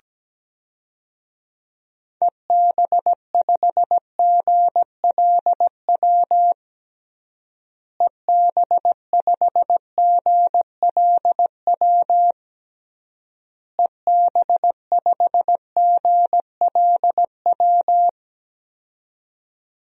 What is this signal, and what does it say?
Morse code, heard as a keyed tone, about 17 wpm, 700 Hz. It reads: EB5GLW EB5GLW EB5GLW
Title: DXCC-100 Call Signs - 6 Characters - Repeated 3x - 17wpm